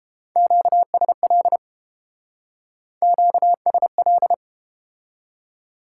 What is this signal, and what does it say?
Morse code, heard as a keyed tone, about 31 wpm, 700 Hz. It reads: QSL QSL